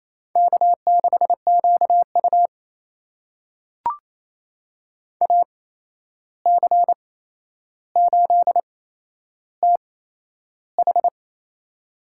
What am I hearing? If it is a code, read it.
Morse code, heard as a keyed tone, about 28 wpm, spaced as for 11 wpm, 700 Hz. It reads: K6QU AC8TH
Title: USA Call Signs - 8x Character Spacing - 28wpm